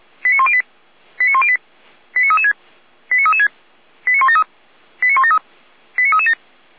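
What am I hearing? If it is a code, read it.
Five-tone selective call (ZVEI), heard as quick runs of five tones, tones of about 80 ms. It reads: 89108 89108 89336 89336 89173 89173 89337
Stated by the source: Five-tone ZVEI selective call received with an SDR, from the audio server of the Feuerwehr Orlamünde fire brigade.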